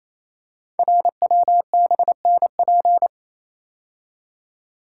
Morse code, heard as a keyed tone, about 28 wpm, 700 Hz. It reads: RWBNP
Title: Code Groups - 5 Letters - 28wpm